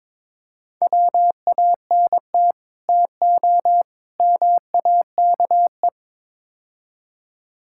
Morse code, heard as a keyed tone, about 22 wpm, 700 Hz. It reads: WANT TO MAKE
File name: practice / 22wpm / Build A Phrase - 5-Word Phrases - 22wpm